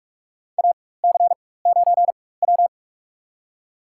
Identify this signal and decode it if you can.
Morse code, heard as a keyed tone, about 46 wpm, spaced as for 21 wpm, 690 Hz. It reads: AC9W
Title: USA Call Signs - 4 Characters - 4x Character Spacing - 45wpm